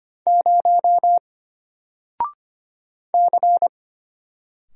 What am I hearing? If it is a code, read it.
Morse code, heard as a keyed tone, about 25 wpm, 700 Hz. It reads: 0 C
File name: Single-Letter-Number-25wpm